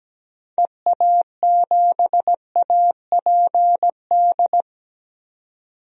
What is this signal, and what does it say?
Morse code, heard as a keyed tone, about 17 wpm, 700 Hz. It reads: EA7APD
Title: DXCC-100 Call Signs - 6 Characters - 17wpm